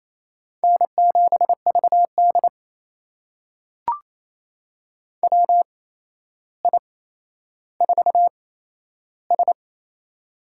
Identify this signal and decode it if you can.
Morse code, heard as a keyed tone, about 27 wpm, spaced as for 12 wpm, 700 Hz. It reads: N7VD WI4S
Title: USA Call Signs - 4 Characters - 8x Character Spacing - 28wpm